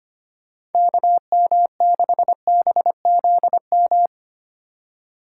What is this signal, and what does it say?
Morse code, heard as a keyed tone, about 25 wpm, 700 Hz. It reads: KM6BZM
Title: USA Call Signs - 6 Characters - 3x Character Spacing - 25wpm